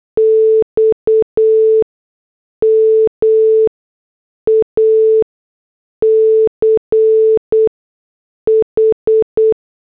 Morse code, heard as a keyed tone, about 8 wpm, 440 Hz. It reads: X M A C H